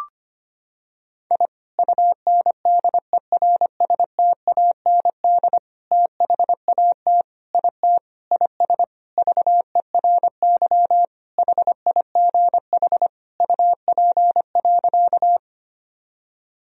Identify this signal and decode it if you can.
Morse code, heard as a keyed tone, about 25 wpm, 700 Hz. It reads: I UNDERSTAND THAT IT IS VERY HIGH UP.
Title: Sentences from Top 1000 Words - Review Part 1 - 25wpm